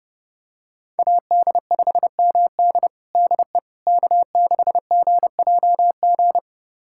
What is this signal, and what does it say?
Morse code, heard as a keyed tone, about 30 wpm, 700 Hz. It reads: AD5MD DE K6GJG
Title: USA Callsign DE Callsign - 5 Characters - 30wpm